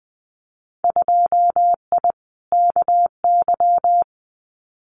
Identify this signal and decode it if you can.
Morse code, heard as a keyed tone, about 20 wpm, 700 Hz. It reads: 2I KY
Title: Winter Field Day Exchanges 20wpm